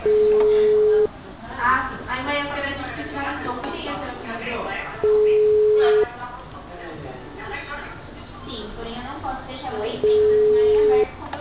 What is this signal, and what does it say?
Ringback tone (Europe, 425 Hz), with speech.
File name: out- -2001- - - .1096